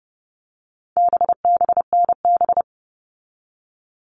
Morse code, heard as a keyed tone, about 30 wpm, 700 Hz. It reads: BBNB
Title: All Cut Numbers - 4-Digit 30wpm